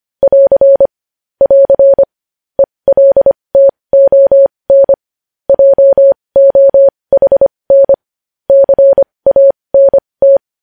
Morse code, heard as a keyed tone, about 25 wpm, spaced as for 20 wpm, 550 Hz. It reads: <AR> <AR> ELTON JOHN CANT
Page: News Headlines in Morse Code at 20 WPM